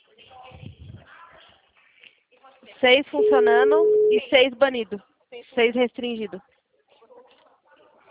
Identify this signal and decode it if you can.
Ringback tone (Europe, 425 Hz), with speech.
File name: out- -2010- - - .58843